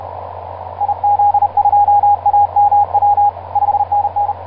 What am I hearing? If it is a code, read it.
Morse code, heard as a keyed tone, about 31 wpm, 820 Hz, with radio noise. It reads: IZ2AMW 5NN
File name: pj2-oh1vr iz2amw etc 12m cw 24mar2010 1955